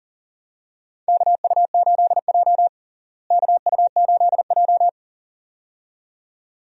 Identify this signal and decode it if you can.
Morse code, heard as a keyed tone, about 37 wpm, 700 Hz. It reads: KU8J KU8J
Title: USA Call Signs - 4 Characters - Repeated 2x - 40wpm